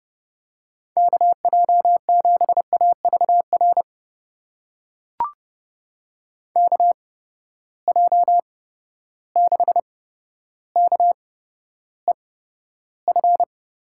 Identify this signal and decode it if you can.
Morse code, heard as a keyed tone, about 32 wpm, spaced as for 13 wpm, 700 Hz. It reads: KJ7AVR KJ6KEF
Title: USA Call Signs - 6 Characters - 8x Character Spacing - 30wpm